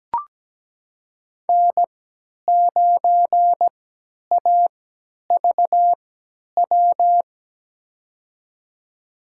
Morse code, heard as a keyed tone, about 17 wpm, spaced as for 10 wpm, 700 Hz. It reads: N9AVW
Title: USA Call Signs - 5 Characters - 3x Character Spacing - 17wpm